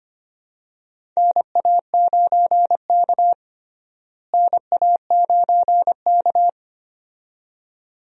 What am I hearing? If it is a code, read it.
Morse code, heard as a keyed tone, about 25 wpm, 700 Hz. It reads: NA9K NA9K
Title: USA Call Signs - 4 Characters - Repeated 2x - 25wpm